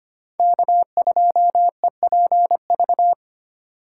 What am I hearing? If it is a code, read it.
Morse code, heard as a keyed tone, about 25 wpm, 700 Hz. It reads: K2EPV
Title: USA Call Signs - 5 Characters - 2x Character Spacing - 25wpm